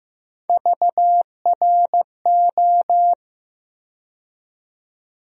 Morse code, heard as a keyed tone, about 15 wpm, 700 Hz. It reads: VRO
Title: USA Call Sign Postfixes - 3 Letters 15wpm